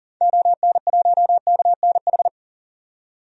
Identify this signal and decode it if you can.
Morse code, heard as a keyed tone, about 40 wpm, 700 Hz. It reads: ON1KNH
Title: DXCC-100 Call Signs - 6 Characters - 40wpm